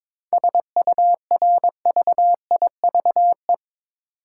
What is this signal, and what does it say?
Morse code, heard as a keyed tone, about 22 wpm, 700 Hz. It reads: SURVIVE
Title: Sentences from Top 1300 Words - New Word Drill - 22wpm